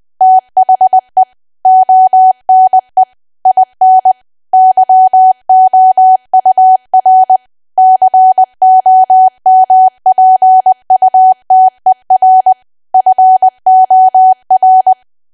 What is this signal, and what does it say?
Morse code, heard as a keyed tone, about 20 wpm, 750 Hz. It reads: THE ONE IN YOUR COMPUTER FOR
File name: cwa / W1AW-Level3 / 20WPM